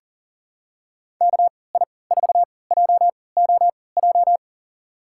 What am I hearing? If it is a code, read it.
Morse code, heard as a keyed tone, about 40 wpm, spaced as for 23 wpm, 700 Hz. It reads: KI4JOJ
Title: USA Call Signs - 6 Characters - 3x Character Spacing - 40wpm